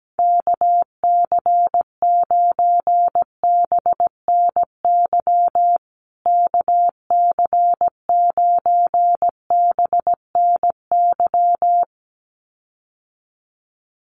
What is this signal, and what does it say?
Morse code, heard as a keyed tone, about 17 wpm, 700 Hz. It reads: KC9BNY KC9BNY
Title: USA Call Signs Repeated 2x 17wpm